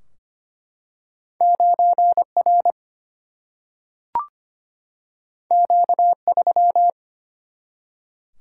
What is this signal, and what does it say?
Morse code, heard as a keyed tone, about 25 wpm, 700 Hz. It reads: 9R Q3
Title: Single-Letter Single-Number 25wpm